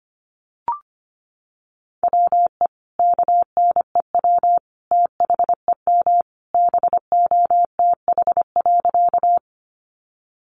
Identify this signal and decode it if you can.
Morse code, heard as a keyed tone, about 25 wpm, 700 Hz. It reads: WE KNEW THEM BOTH.
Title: Sentences from Top 400 Words 25wpm